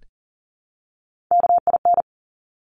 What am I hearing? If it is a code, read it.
Morse code, heard as a keyed tone, about 42 wpm, 700 Hz. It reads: KIN
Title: Top Three Letter Combinations in Words 40wpm